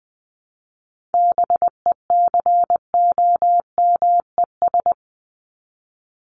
Morse code, heard as a keyed tone, about 20 wpm, 700 Hz. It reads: BECOMES
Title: ES:S Suffix - Top 100 Words with Base Words 20wpm